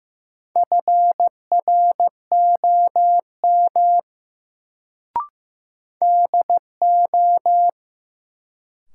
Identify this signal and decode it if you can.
Morse code, heard as a keyed tone, about 15 wpm, 700 Hz. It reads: FROM DO
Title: Sets of 2 Words - New Word Drill - 15wpm